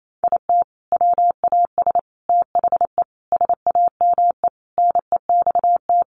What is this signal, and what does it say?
Morse code, heard as a keyed tone, about 28 wpm, 700 Hz. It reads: IT WAS THE SAME NEXT
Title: Sentences from Top 500 Words 28wpm